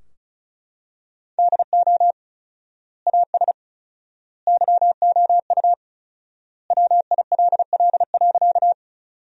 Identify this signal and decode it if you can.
Morse code, heard as a keyed tone, about 34 wpm, 700 Hz. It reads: DO AS YOU WILL.
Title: 4x Word-Spacing - Sentences from Top 100 Words - 35wpm